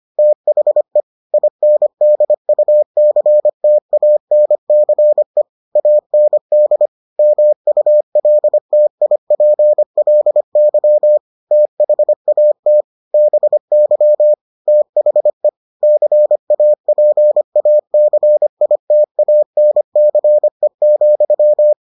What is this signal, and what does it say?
Morse code, heard as a keyed tone, about 25 wpm, 600 Hz. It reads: THE INDUCTANCE AND MULTIPLY THAT BY THE CAPACITANCE,